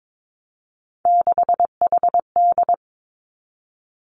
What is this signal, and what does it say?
Morse code, heard as a keyed tone, about 22 wpm, 700 Hz. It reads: 6HD
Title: Three Challenging Letter-Number Combinations 22wpm